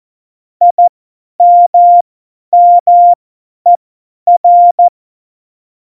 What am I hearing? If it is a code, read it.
Morse code, heard as a keyed tone, about 14 wpm, spaced as for 10 wpm, 700 Hz. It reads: IMMER
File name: Morsekurs / Witze / Witz 01. Dezember 2020, 50 F70